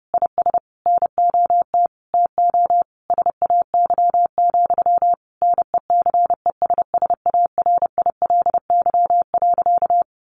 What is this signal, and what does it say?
Morse code, heard as a keyed tone, about 30 wpm, 700 Hz. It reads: IS NOT TO SAY, NECESSARILY.
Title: Sentences from Top 1400 Words 30wpm